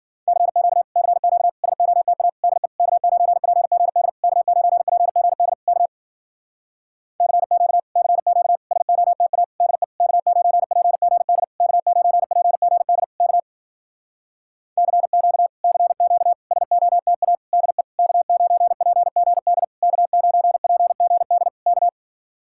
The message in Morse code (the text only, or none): CQ CQ SOTA DE K9PGD K9PGD K CQ CQ SOTA DE K9PGD K9PGD K CQ CQ SOTA DE K9PGD K9PGD K